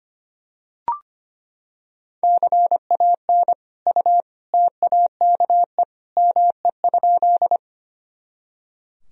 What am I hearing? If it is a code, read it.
Morse code, heard as a keyed tone, about 25 wpm, 700 Hz. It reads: CAN U TAKE ME?
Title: Sentences from Top 200 Words and Abbreviations 25wpm